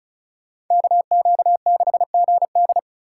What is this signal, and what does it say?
Morse code, heard as a keyed tone, about 46 wpm, 700 Hz. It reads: KQ6GD